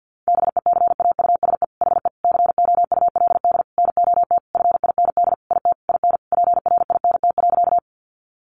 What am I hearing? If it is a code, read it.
Morse code, heard as a keyed tone, about 50 wpm, 700 Hz. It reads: BECAUSE HE COULD NOT FIND IT IN PRINT.
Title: Sentences from Top 1400 Words 50wpm